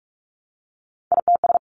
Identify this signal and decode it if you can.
Morse code, heard as a keyed tone, about 44 wpm, 700 Hz. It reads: ITS
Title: Sentences from Top 600 Words 45wpm